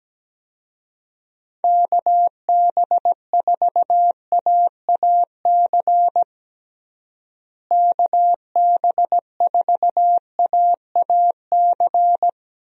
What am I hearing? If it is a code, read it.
Morse code, heard as a keyed tone, about 17 wpm, 700 Hz. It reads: KB4AAC KB4AAC
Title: USA Call Signs - 6 Characters - Repeated 2x - 17wpm